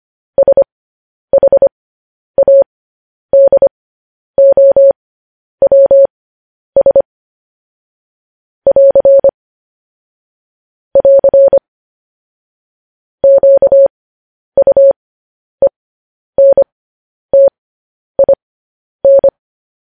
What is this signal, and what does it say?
Morse code, heard as a keyed tone, about 25 wpm, spaced as for 10 wpm, 550 Hz. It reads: SHADOWS <AR> <AR> QUENTIN